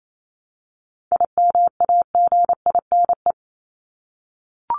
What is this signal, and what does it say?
Morse code, heard as a keyed tone, about 28 wpm, 700 Hz. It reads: IMAGINE
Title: Top 1251-1500 Words 28wpm